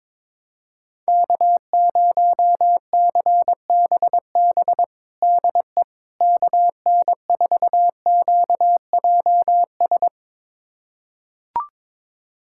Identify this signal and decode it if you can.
Morse code, heard as a keyed tone, about 22 wpm, 700 Hz. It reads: K0CBB DE KN4QJS
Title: USA Callsign DE Callsign 22wpm